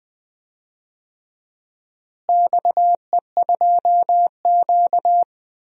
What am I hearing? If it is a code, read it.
Morse code, heard as a keyed tone, about 21 wpm, 700 Hz. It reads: XE2Q
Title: DXCC-100 Call Signs - 4 Characters - Repeated 3x - 20wpm